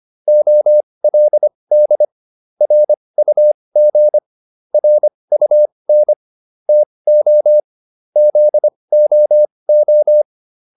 Morse code, heard as a keyed tone, about 25 wpm, spaced as for 20 wpm, 600 Hz. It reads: OLD RUG RUN TO ZOO